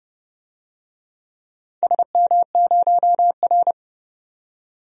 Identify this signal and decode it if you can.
Morse code, heard as a keyed tone, about 30 wpm, 700 Hz. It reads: SM0R
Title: DXCC-100 Call Signs - 4 Characters - 30wpm